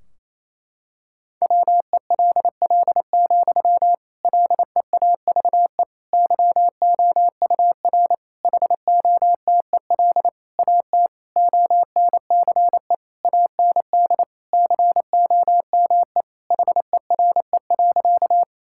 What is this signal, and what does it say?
Morse code, heard as a keyed tone, about 28 wpm, 700 Hz. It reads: WELL, LEAVE YOUR HOTEL AT ONCE AND COME HERE.
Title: Sentences from Top 1600 Words 28wpm